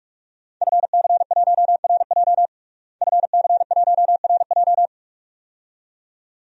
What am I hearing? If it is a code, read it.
Morse code, heard as a keyed tone, about 46 wpm, 700 Hz. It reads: FC1RJ FC1RJ
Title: DXCC-100 Call Signs - 5 Characters - Repeated 2x - 45wpm